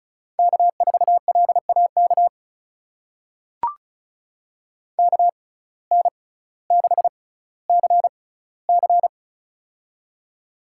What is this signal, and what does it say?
Morse code, heard as a keyed tone, about 34 wpm, spaced as for 18 wpm, 700 Hz. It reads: K4LAK KN6CC
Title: USA Call Signs - 5 Characters - 6x Character Spacing - 35wpm